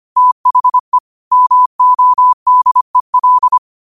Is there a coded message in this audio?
Yes: Morse code from 0.2 to 3.6 s, heard as a keyed tone, about 25 wpm, 1000 Hz. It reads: THE MODEL